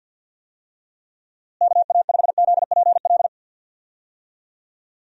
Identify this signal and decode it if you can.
Morse code, heard as a keyed tone, about 50 wpm, 700 Hz. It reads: KA5ZPL
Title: USA Call Signs - 6 Characters - Repeated 2x - 50wpm